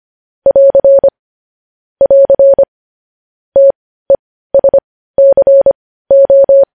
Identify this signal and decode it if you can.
Morse code, heard as a keyed tone, about 25 wpm, spaced as for 15 wpm, 550 Hz. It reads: <AR> <AR> TESCO
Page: news headlines in morse code at 15 wpm